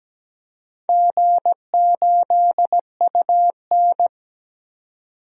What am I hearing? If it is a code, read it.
Morse code, heard as a keyed tone, about 17 wpm, 700 Hz. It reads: G8UN
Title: DXCC-100 Call Signs - 4 Characters - 17wpm